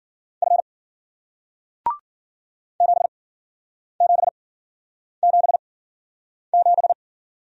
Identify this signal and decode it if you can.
Morse code, heard as a keyed tone, about 54 wpm, spaced as for 9 wpm, 700 Hz. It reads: F 7777